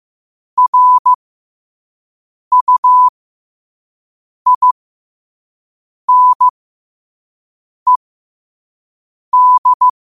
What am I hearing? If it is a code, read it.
Morse code, heard as a keyed tone, about 15 wpm, spaced as for 5 wpm, 1000 Hz. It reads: RUINED
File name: qotdfiles / 5wpm